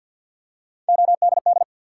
Morse code, heard as a keyed tone, about 52 wpm, 700 Hz. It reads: ODD
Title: Three Letter Words - 50wpm